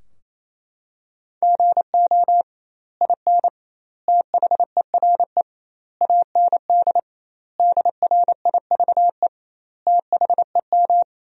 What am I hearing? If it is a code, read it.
Morse code, heard as a keyed tone, about 28 wpm, 700 Hz. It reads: GO IN THERE AND DRIVE THEM
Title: 2x Word-Spacing - Sentences from Top 500 Words - 28wpm